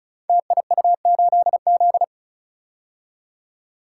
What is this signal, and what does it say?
Morse code, heard as a keyed tone, about 34 wpm, 700 Hz. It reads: TIU8Z
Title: Code Groups - 5 Letter:Numbers - 35wpm